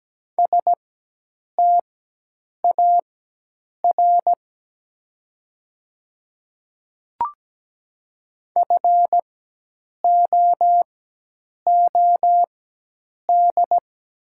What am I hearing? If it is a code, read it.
Morse code, heard as a keyed tone, about 17 wpm, spaced as for 8 wpm, 700 Hz. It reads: STAR FOOD
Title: 4x Character Spacing - Top 500 Words - 17wpm